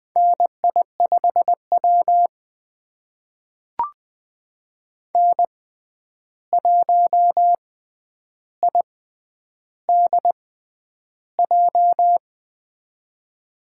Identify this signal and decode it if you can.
Morse code, heard as a keyed tone, about 20 wpm, spaced as for 10 wpm, 700 Hz. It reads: NI5W N1IDJ
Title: USA Call Signs - 6x Character Spacing - 20wpm